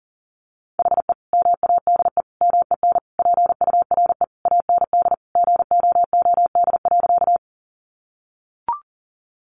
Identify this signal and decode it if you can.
Morse code, heard as a keyed tone, about 40 wpm, 700 Hz. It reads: HE MADE MEN PURE AND GOOD.
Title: Sentences from Top 1500 Words 40wpm